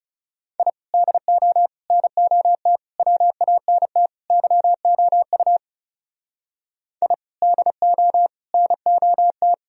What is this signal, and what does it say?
Morse code, heard as a keyed tone, about 38 wpm, 700 Hz. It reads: I DO NOT WANT YOU I DO NOT
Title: Speed-Racing - Sets of 5 Words 30wpm